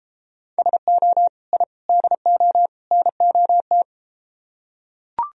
Morse code, heard as a keyed tone, about 31 wpm, 700 Hz. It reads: SO I DO NOT